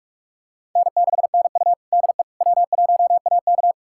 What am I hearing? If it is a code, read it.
Morse code, heard as a keyed tone, about 41 wpm, 700 Hz. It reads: N6NU DE W1AK
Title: USA Callsign DE Callsign - 4 Characters - 45wpm